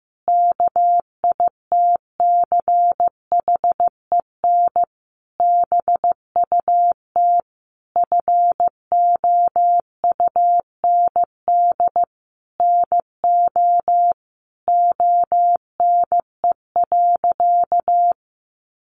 Morse code, heard as a keyed tone, about 15 wpm, 700 Hz. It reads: KITCHEN BUT FOUND NO ONE.
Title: Sentences from Top 1400 Words 15wpm